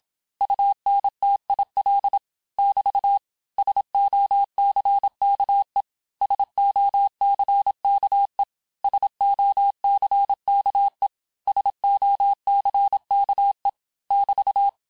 Morse code, heard as a keyed tone, about 27 wpm, 800 Hz. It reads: UNTIL <BT> SOCKE SOCKE SOCKE SOCKE <BT>
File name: cours-du-mercredi / audios / cours-869-texte-LES-MOTS-vitesse-25-MPM